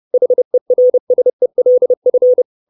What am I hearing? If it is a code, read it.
Morse code, heard as a keyed tone, about 30 wpm, 500 Hz. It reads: HERSELF
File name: 632 Words 30WPM